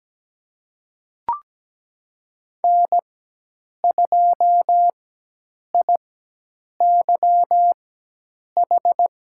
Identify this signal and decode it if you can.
Morse code, heard as a keyed tone, about 17 wpm, spaced as for 8 wpm, 700 Hz. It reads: N2IYH
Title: USA Call Signs - 5 Characters - 4x Character Spacing - 17wpm